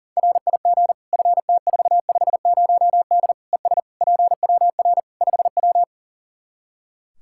Morse code, heard as a keyed tone, about 40 wpm, 700 Hz. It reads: RIG FT450D ES PWR 5W
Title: Practice QSO - Exchange 2 - Info - 40wpm